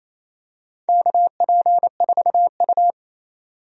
Morse code, heard as a keyed tone, about 28 wpm, 700 Hz. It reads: KP4U
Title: DXCC-100 Call Signs - 4 Characters - 28wpm